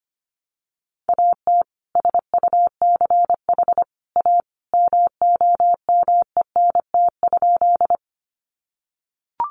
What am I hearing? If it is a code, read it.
Morse code, heard as a keyed tone, about 25 wpm, 700 Hz. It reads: AT SUCH A MOMENT?
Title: Sentences from Top 1000 Words 25wpm